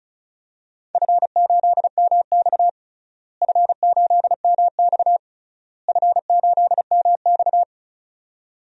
Morse code, heard as a keyed tone, about 36 wpm, 700 Hz. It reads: F8MX F8MX F8MX
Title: DXCC-100 Call Signs - 4 Characters - Repeated 3x - 35wpm